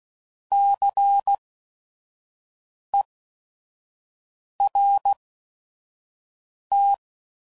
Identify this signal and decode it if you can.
Morse code, heard as a keyed tone, about 16 wpm, spaced as for 5 wpm, 800 Hz. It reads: CERT
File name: Prefixes / By First Letter / Cs